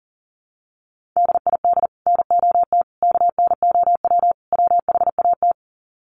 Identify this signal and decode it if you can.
Morse code, heard as a keyed tone, about 40 wpm, 700 Hz. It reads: DID NOT KNOW WHAT
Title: Sets of 4 Words 40wpm